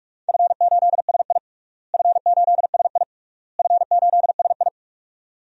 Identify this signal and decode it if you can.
Morse code, heard as a keyed tone, about 44 wpm, 700 Hz. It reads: F8SI F8SI F8SI
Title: DXCC-100 Call Signs - 4 Characters - Repeated 3x - 45wpm